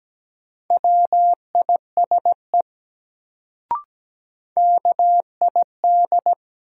Morse code, heard as a keyed tone, about 17 wpm, 700 Hz. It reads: WISE KID
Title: Sentences from Top 1600 Words - New Word Drill - 17wpm